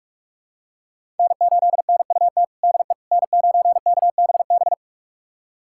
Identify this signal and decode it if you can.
Morse code, heard as a keyed tone, about 46 wpm, 700 Hz. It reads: N8NUT DE N9KBB
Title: USA Callsign DE Callsign - 5 Characters - 45wpm